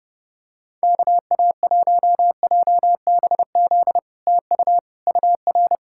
Morse code, heard as a keyed tone, about 30 wpm, 700 Hz. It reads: KA1JBZ TU UR
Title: SOTA QSO Practice 30wpm